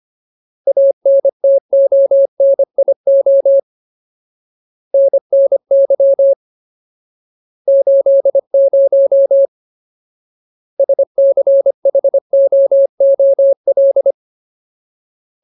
Morse code, heard as a keyed tone, about 25 wpm, 550 Hz. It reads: ANTONIO NNY 80 SCHOOL
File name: Ultimate Word List / 25WPM Ultimate